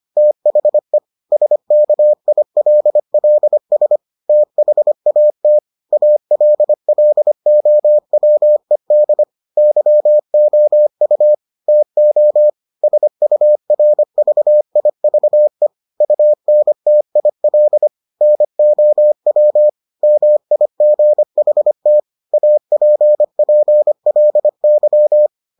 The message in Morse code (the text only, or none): THE SKILLS THAT ALLOWED YOU TO SURVIVE UNTIL NOW MIGHT APPLY